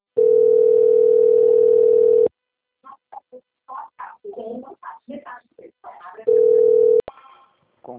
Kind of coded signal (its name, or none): telephone call progress tone